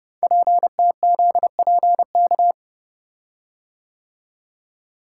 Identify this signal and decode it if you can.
Morse code, heard as a keyed tone, about 29 wpm, 700 Hz. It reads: PTZPK
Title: Code Groups - 5 Letters - 30wpm